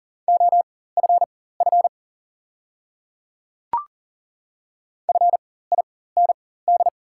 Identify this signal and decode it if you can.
Morse code, heard as a keyed tone, about 40 wpm, spaced as for 19 wpm, 700 Hz. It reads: OFF FIND
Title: 4x Character Spacing - Top 200 Words - 40wpm